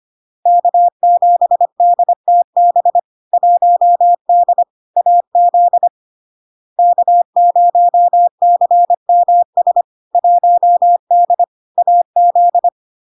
Morse code, heard as a keyed tone, about 25 wpm, 700 Hz. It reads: K7DTB 1D AZ K0CMS 1D AZ
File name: LARK-FD-2022-Log-in-CW-25wpm-700-Hz